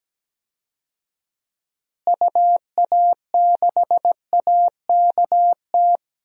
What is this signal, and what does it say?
Morse code, heard as a keyed tone, about 17 wpm, 700 Hz. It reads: UA6AKT
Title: DXCC-100 Call Signs - 6 Characters - Repeated 3x - 17wpm